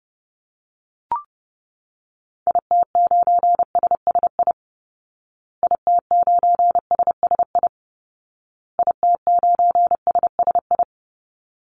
Morse code, heard as a keyed tone, about 30 wpm, 700 Hz. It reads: IT9SSI IT9SSI IT9SSI
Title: Call Signs Repeated 2x - 30wpm